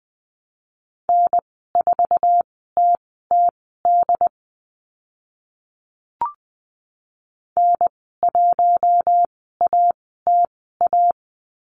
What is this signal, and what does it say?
Morse code, heard as a keyed tone, about 20 wpm, spaced as for 14 wpm, 700 Hz. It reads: N4TTD N1ATA
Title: 076 Lesson 13g - Letter L - Call Signs 20wpm